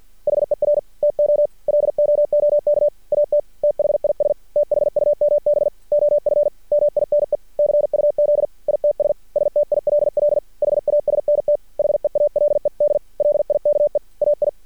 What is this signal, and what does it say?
Morse code, heard as a keyed tone, about 50 wpm, 590 Hz. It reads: HER TO LOOK AT THIS THUMB OF MINE CUZ ITS STILL HASNT HEALED LIKE AI
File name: archived / CRSnet